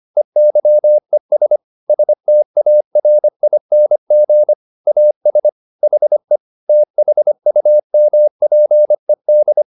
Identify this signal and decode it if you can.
Morse code, heard as a keyed tone, about 25 wpm, 600 Hz. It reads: EYES STARING AS HE THUMPED
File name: the old man and the sea part 1 25wpm